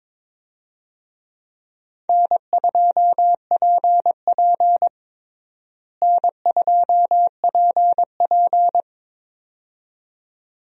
Morse code, heard as a keyed tone, about 22 wpm, 700 Hz. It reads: N2PP N2PP
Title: USA Call Signs - 4 Characters - Repeated 2x - 22wpm